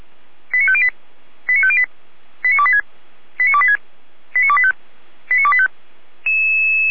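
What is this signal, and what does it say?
Five-tone selective call (ZVEI), heard as quick runs of five tones, tones of about 70 ms. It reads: 89508 89508 89286 89286 89285 89285
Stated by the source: ZVEI five-tone selective call received by SDR, code 89305, archived by Feuerwehr Orlamünde.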